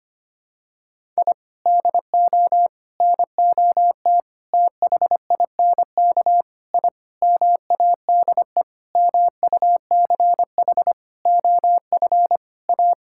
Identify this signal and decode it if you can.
Morse code, heard as a keyed tone, about 25 wpm, 700 Hz. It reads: I DO NOT THINK I MADE MUCH OF A
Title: Sentences from Top 1000 Words - Review Part 6 - 25wpm